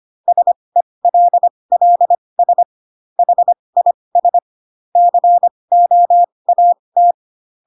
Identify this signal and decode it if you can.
Morse code, heard as a keyed tone, about 25 wpm, spaced as for 20 wpm, 700 Hz. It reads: SELLS HIS COAT
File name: Pinocchio / Pinocchio ch8 15 25 ik1hge